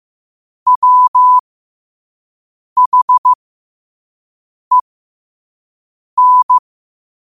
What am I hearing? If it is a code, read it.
Morse code, heard as a keyed tone, about 15 wpm, spaced as for 5 wpm, 1000 Hz. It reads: WHEN
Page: Quotes for Fri, 15 Aug 2025 in Morse Code at 5 words per minute.